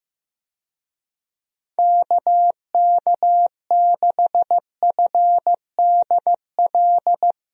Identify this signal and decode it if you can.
Morse code, heard as a keyed tone, about 15 wpm, 700 Hz. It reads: KK6FDL
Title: USA Call Signs - 6 Characters - Repeated 3x - 15wpm